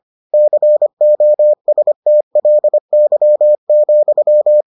Morse code, heard as a keyed tone, about 25 wpm, 600 Hz. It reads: COSTLY,